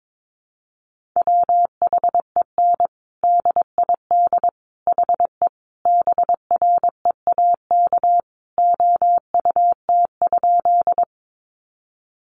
Morse code, heard as a keyed tone, about 22 wpm, 700 Hz. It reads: WHEN DID HE BREAK OUT?
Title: Sentences from Top 900 Words 22wpm